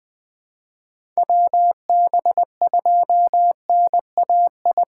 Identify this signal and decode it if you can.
Morse code, heard as a keyed tone, about 20 wpm, 700 Hz. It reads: WB2NAI
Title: USA Call Signs - 6 Characters - 6x Character Spacing - 20wpm